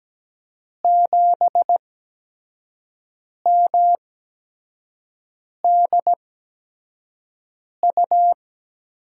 Morse code, heard as a keyed tone, about 17 wpm, spaced as for 5 wpm, 700 Hz. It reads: 7MDU
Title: USA Call Signs - 6 Characters - 8x Character Spacing - 17wpm